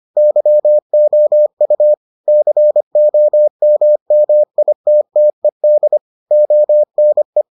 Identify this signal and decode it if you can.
Morse code, heard as a keyed tone, about 25 wpm, 600 Hz. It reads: YOU COMMITTED ONE